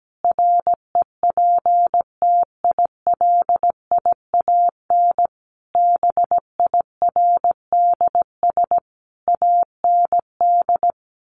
Morse code, heard as a keyed tone, about 17 wpm, 700 Hz. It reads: REPTILIAN BIRDS AND
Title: A Sound Of Thunder 17wpm